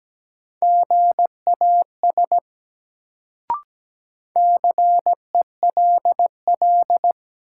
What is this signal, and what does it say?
Morse code, heard as a keyed tone, about 17 wpm, 700 Hz. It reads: GAS CELL
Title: Sentences from Top 1100 Words - New Word Drill - 17wpm